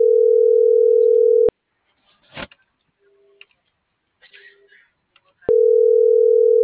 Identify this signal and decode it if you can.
Ringback tone (North America), 440 + 480 Hz.
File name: CENTRAL MULTIMODAL / out josimar.cruz - .58826